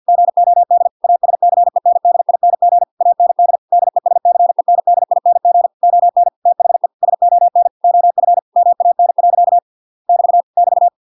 Morse code, heard as a keyed tone, about 50 wpm, 700 Hz. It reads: GOD ASCENDING AND DESCENDING ON THE SON OF MAN. <BT> <BT>